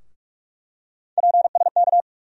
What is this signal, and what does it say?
Morse code, heard as a keyed tone, about 41 wpm, 700 Hz. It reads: PSK
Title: USA Call Sign Postfixes - 3 Letters 45wpm